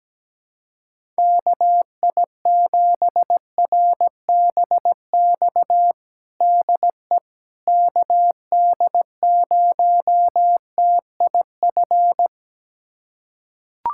Morse code, heard as a keyed tone, about 17 wpm, 700 Hz. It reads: KI7RBX DE KD0TIF
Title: USA Callsign DE Callsign - 6 Characters - 17wpm